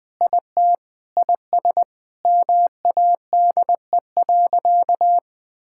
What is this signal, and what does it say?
Morse code, heard as a keyed tone, about 21 wpm, 700 Hz. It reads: IT IS MADE.
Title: Sentences with 5-Word Phrases - 20wpm